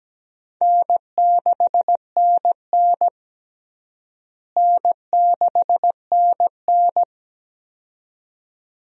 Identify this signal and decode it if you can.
Morse code, heard as a keyed tone, about 17 wpm, 700 Hz. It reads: N6NN N6NN
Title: USA Call Signs - 4 Characters - Repeated 2x - 17wpm